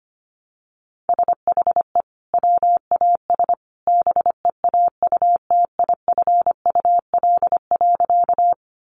Morse code, heard as a keyed tone, about 25 wpm, 700 Hz. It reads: SHE WAS BEAUTIFUL.
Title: Sentences from Top 900 Words 25wpm